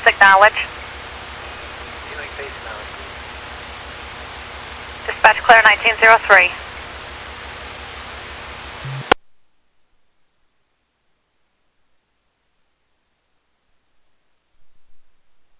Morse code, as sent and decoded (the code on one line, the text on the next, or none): none